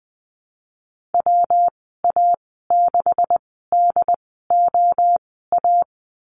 Morse code, heard as a keyed tone, about 20 wpm, spaced as for 15 wpm, 700 Hz. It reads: WA6DOA
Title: 167 Lesson 27g - Number 7 - Call Signs 20wpm